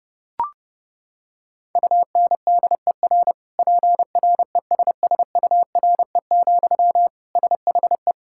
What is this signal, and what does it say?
Morse code, heard as a keyed tone, about 32 wpm, 700 Hz. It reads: UNDER PRESSURE, SHE
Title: Sentences from Top 1000 Words - Review Part 6 - 30wpm